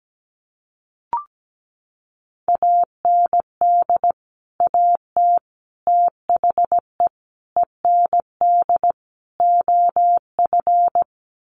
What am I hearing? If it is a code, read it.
Morse code, heard as a keyed tone, about 17 wpm, 700 Hz. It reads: AND AT THE END OF
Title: Sets of 5 Words 17wpm